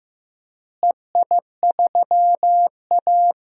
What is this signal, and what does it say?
Morse code, heard as a keyed tone, about 15 wpm, 700 Hz. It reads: EI3A
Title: DXCC-100 Call Signs - 4 Characters - Repeated 2x - 15wpm